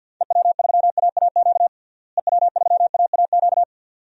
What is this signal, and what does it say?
Morse code, heard as a keyed tone, about 52 wpm, 700 Hz. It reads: EW3AAQ EW3AAQ
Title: DXCC-100 Call Signs - 6 Characters - Repeated 2x - 50wpm